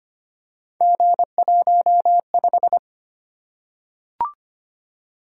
Morse code, heard as a keyed tone, about 25 wpm, 700 Hz. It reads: G15
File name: Three Letter-Number Combinations - 25wpm